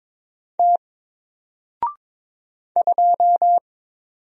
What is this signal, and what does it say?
Morse code, heard as a keyed tone, about 22 wpm, 700 Hz. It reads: T 2
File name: Single-Letter-Number-22wpm